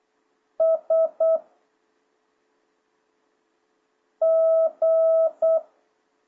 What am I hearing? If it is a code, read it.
Morse code, heard as a keyed tone, about 8 wpm, 650 Hz. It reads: S G